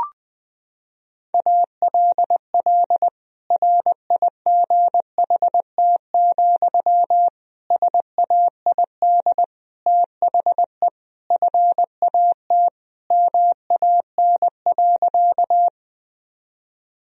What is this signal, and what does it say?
Morse code, heard as a keyed tone, about 20 wpm, 700 Hz. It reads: ALL RIGHT, SAID THE FAT MAN.
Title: Sentences from Top 1000 Words - Review Part 4 - 20wpm